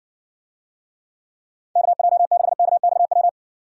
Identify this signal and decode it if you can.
Morse code, heard as a keyed tone, about 60 wpm, 700 Hz. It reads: KJ6KXW